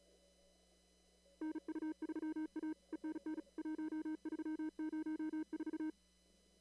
Morse code, heard as a keyed tone, about 36 wpm, 310 Hz. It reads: NU3A ENN 1304
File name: CW / kt5j wpx cw-op-k5pi